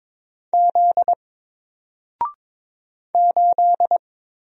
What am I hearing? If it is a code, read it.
Morse code, heard as a keyed tone, about 22 wpm, 700 Hz. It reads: Z 8
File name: Single-Letter-Number-22wpm